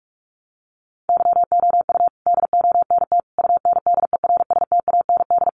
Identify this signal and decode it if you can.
Morse code, heard as a keyed tone, about 44 wpm, 700 Hz. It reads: YOU DONT UNDERSTAND
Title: Sentences from Top 1000 Words 45wpm